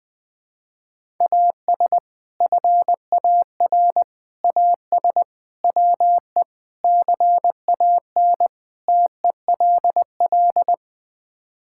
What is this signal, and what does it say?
Morse code, heard as a keyed tone, about 20 wpm, 700 Hz. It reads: AS FAR AS WE CAN TELL
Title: Sets of 6 Words - 20wpm